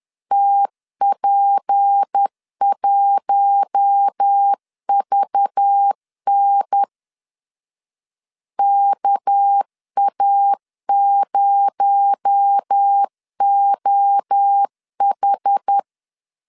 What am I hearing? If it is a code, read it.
Morse code, heard as a keyed tone, about 10 wpm, 800 Hz. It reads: TP1VN KA0OH
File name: CW / DET / 1 6 1500C S B 10wpm 4137